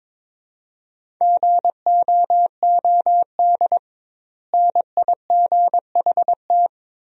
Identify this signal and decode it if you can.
Morse code, heard as a keyed tone, about 22 wpm, 700 Hz. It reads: GOOD NIGHT
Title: 2x Word-Spacing - Sentences from Top 300 Words - 22wpm